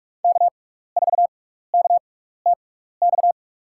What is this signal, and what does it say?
Morse code, heard as a keyed tone, about 47 wpm, spaced as for 16 wpm, 690 Hz. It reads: K4KTX
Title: USA Call Signs - 5 Characters - 6x Character Spacing - 45wpm